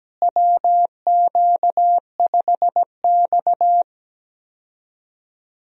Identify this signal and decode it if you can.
Morse code, heard as a keyed tone, about 17 wpm, 700 Hz. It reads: WQ5X
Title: USA Call Signs - 4 Characters - Repeated 2x - 17wpm